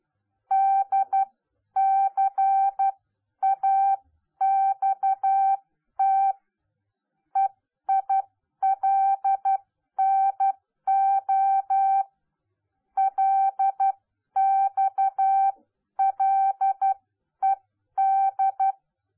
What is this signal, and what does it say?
Morse code, heard as a keyed tone, about 12 wpm, spaced as for 10 wpm, 780 Hz. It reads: DCAXT EILNO LXLED